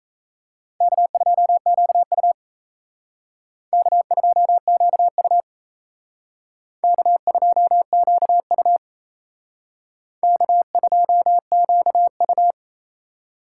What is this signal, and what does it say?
Morse code, heard as a keyed tone, about 34 wpm, 700 Hz. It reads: K2QU K2QU K2QU K2QU
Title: Speed Racing - USA 1x2 Call Signs 28wpm